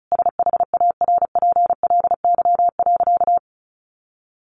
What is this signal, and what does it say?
Morse code, heard as a keyed tone, about 34 wpm, 700 Hz. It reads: SHARPLY.
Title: A Sound Of Thunder 35wpm